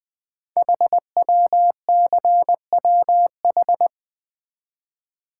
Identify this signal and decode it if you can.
Morse code, heard as a keyed tone, about 20 wpm, 700 Hz. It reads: HWCWH
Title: Code Groups - 5 Letters - 20wpm